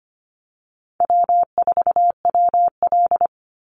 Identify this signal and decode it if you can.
Morse code, heard as a keyed tone, about 25 wpm, 700 Hz. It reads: W4WL
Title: USA Call Signs 25wpm